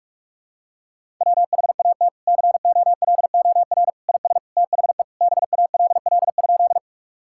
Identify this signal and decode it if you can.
Morse code, heard as a keyed tone, about 41 wpm, 700 Hz. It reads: WHAT COLOR IS THE BALL?
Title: Sentences from Top 1000 Words - Review Part 6 - 45wpm